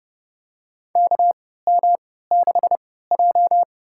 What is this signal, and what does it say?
Morse code, heard as a keyed tone, about 30 wpm, spaced as for 17 wpm, 700 Hz. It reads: KM6J
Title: USA Call Signs - 4 Characters - 3x Character Spacing - 30wpm